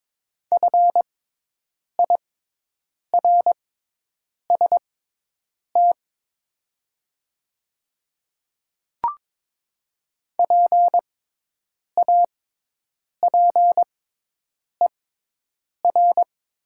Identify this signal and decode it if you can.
Morse code, heard as a keyed tone, about 22 wpm, spaced as for 8 wpm, 700 Hz. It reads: FIRST PAPER
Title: 6x Character Spacing - Top 300 Words - 22wpm